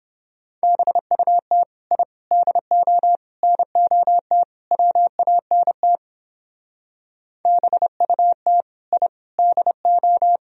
Morse code, heard as a keyed tone, about 29 wpm, 700 Hz. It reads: BUT I DO NOT WANT BUT I DO